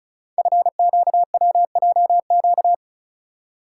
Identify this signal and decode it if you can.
Morse code, heard as a keyed tone, about 36 wpm, 700 Hz. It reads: FQWJQ
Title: Code Groups - 5 Letters - 35wpm